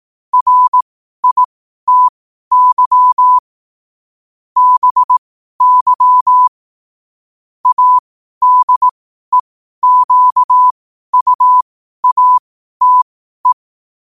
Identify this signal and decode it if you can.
Morse code, heard as a keyed tone, about 18 wpm, spaced as for 12 wpm, 1000 Hz. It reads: RITY BY ADEQUATE
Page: New quotes every day in morse code at 12 Words per minute.